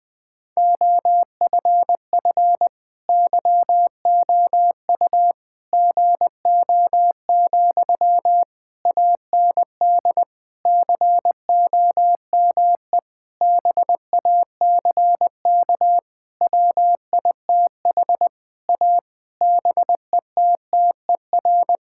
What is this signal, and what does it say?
Morse code, heard as a keyed tone, about 21 wpm, 700 Hz. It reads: OFF YOU GO, AND COME BACK WITH A BETTER
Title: Sentences from Top 1000 Words - Review Part 6 - 20wpm